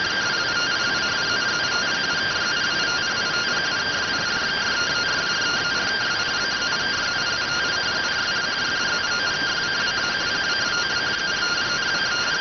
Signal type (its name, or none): two-tone teleprinter signal